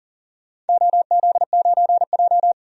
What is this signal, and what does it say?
Morse code, heard as a keyed tone, about 40 wpm, 700 Hz. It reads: OZ9J
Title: DXCC-100 Call Signs - 4 Characters - 40wpm